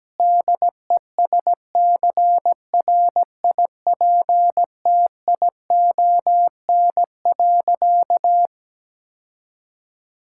Morse code, heard as a keyed tone, about 17 wpm, 700 Hz. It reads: DESCRIPTION.
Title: Sentences from Top 1600 Words 17wpm